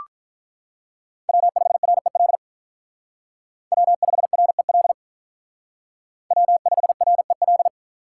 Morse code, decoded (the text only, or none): W5REL W5REL W5REL